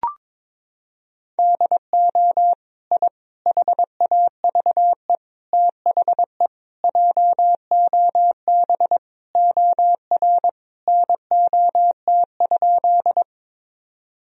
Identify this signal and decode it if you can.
Morse code, heard as a keyed tone, about 22 wpm, 700 Hz. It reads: DO I HAVE THE JOB OR NOT?
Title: Sentences from Top 1000 Words - Review Part 3 - 22wpm